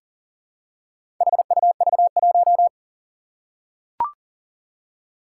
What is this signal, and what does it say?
Morse code, heard as a keyed tone, about 41 wpm, 700 Hz. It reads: HUV1
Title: Four Challenging Letter-Number Combinations 40wpm